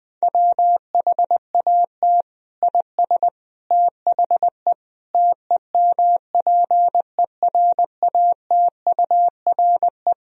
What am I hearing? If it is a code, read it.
Morse code, heard as a keyed tone, about 20 wpm, 700 Hz. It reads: WHAT IS THE TEMPERATURE
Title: Sentences from Top 1000 Words - Review Part 4 - 20wpm